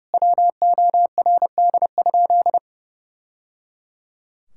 Morse code, heard as a keyed tone, about 30 wpm, 700 Hz. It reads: WORD?
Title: Sentences from Top 1000 Words - Review Part 6 - 30wpm